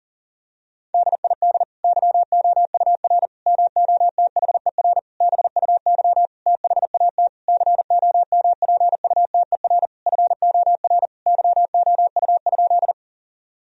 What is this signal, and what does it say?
Morse code, heard as a keyed tone, about 37 wpm, 700 Hz. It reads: DID YOUR MOTHER BUY THAT COMPUTER FOR YOU?
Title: Sentences from Top 1000 Words - Review Part 3 - 40wpm